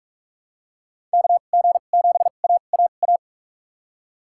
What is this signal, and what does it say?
Morse code, heard as a keyed tone, about 46 wpm, spaced as for 32 wpm, 700 Hz. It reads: KG7AAA
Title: USA Call Signs - 6 Characters - 2x Character Spacing - 45wpm